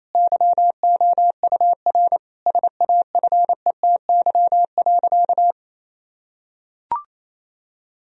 Morse code, heard as a keyed tone, about 28 wpm, 700 Hz. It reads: YOUR SAFETY.
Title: Sentences from Top 1000 Words - Review Part 4 - 28wpm